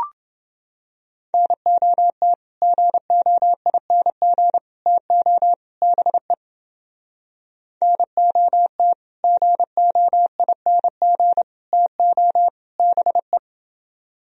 Morse code, heard as a keyed tone, about 30 wpm, 700 Hz. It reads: NOT GOING TO BE NOT GOING TO BE